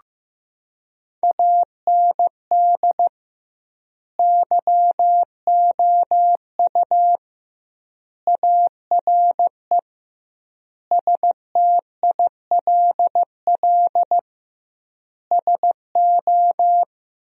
Morse code, heard as a keyed tone, about 16 wpm, 700 Hz. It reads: AND YOU ARE STILL SO
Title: 2x Word-Spacing - Sentences from Top 400 Words - 15wpm